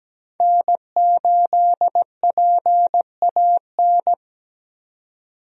Morse code, heard as a keyed tone, about 17 wpm, 700 Hz. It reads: N8PAN
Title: USA Call Signs - 5 Characters - 17wpm